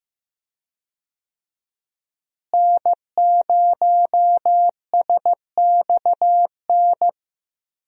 Morse code, heard as a keyed tone, about 15 wpm, 700 Hz. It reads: N0SXN
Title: USA Call Signs - 5 Characters - Repeated 2x - 15wpm